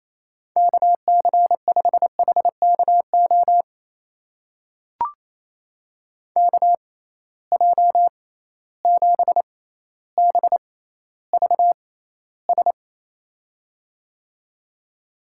Morse code, heard as a keyed tone, about 27 wpm, spaced as for 14 wpm, 700 Hz. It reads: KC5HKO KJ7BVS
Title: USA Call Signs - 6 Characters - 6x Character Spacing - 28wpm